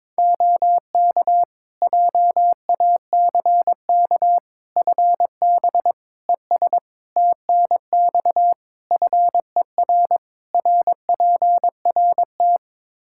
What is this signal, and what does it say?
Morse code, heard as a keyed tone, about 21 wpm, 700 Hz. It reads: OK JACK FB ES TNX FER RPRT
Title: Practice QSO - Exchange 2 - Info - 22wpm